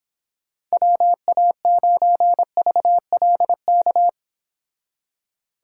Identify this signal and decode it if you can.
Morse code, heard as a keyed tone, about 31 wpm, 700 Hz. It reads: WA9VLK